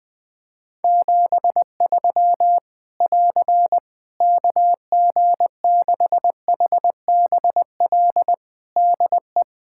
Morse code, heard as a keyed tone, about 20 wpm, 700 Hz. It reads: 73 <AR> KG6HBL DE
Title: Practice QSO - Exchange 4 - Ending - 20wpm